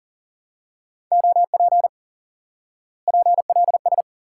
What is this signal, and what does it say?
Morse code, heard as a keyed tone, about 39 wpm, 700 Hz. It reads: OP PLS
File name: IQE Course 3 - Lesson 89 - YAGI - Two Elements - 40wpm